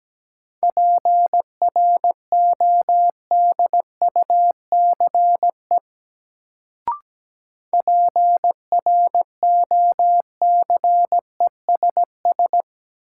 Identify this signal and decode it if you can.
Morse code, heard as a keyed tone, about 17 wpm, 700 Hz. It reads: PRODUCE PROCESS
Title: Sentences from Top 1000 Words - Review Part 3 - New Word Drill - 17wpm